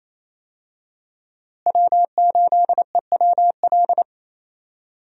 Morse code, heard as a keyed tone, about 27 wpm, 700 Hz. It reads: W8EWL
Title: USA Call Signs - 5 Characters - Repeated 3x - 28wpm